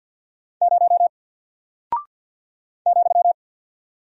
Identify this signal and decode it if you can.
Morse code, heard as a keyed tone, about 48 wpm, 700 Hz. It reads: 0 ,
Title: Single Character (Letter:Number:Punctuation:Prosign) 50wpm